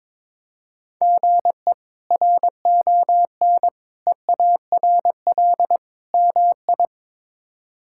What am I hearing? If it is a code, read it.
Morse code, heard as a keyed tone, about 21 wpm, 700 Hz. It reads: GE RON EARL MI
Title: K1USN SST QSO Practice - 22wpm